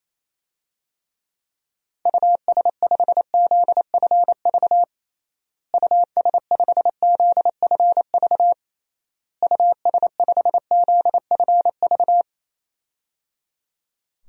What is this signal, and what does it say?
Morse code, heard as a keyed tone, about 28 wpm, 700 Hz. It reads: US5ZFV US5ZFV US5ZFV
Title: DXCC-100 Call Signs - 6 Characters - Repeated 3x - 28wpm